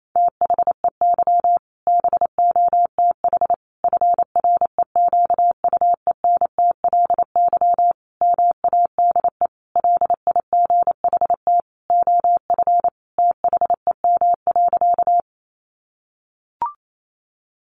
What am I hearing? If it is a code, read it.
Morse code, heard as a keyed tone, about 28 wpm, 700 Hz. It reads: THEY BOTH FREQUENTLY MADE LIGHT OF THEM.
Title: Sentences from Top 1300 Words 28wpm